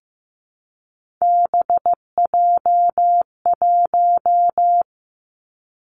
Morse code, heard as a keyed tone, about 15 wpm, 700 Hz. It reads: BJ1
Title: Three Challenging Letter-Number Combinations 15wpm